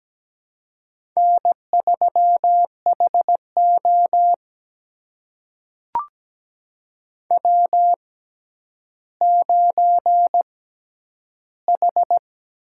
Morse code, heard as a keyed tone, about 17 wpm, spaced as for 10 wpm, 700 Hz. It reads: N3HO W9H
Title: USA Call Signs - 4 Characters - 6x Character Spacing - 17wpm